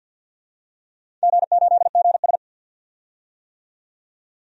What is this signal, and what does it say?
Morse code, heard as a keyed tone, about 51 wpm, 700 Hz. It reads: G8GS
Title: DXCC-100 Call Signs - 4 Characters - 50wpm